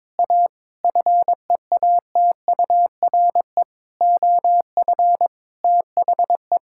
Morse code, heard as a keyed tone, about 22 wpm, 700 Hz. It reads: A FEATURE OF THE
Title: Sentences from Top 1000 Words - Review Part 5 - 22wpm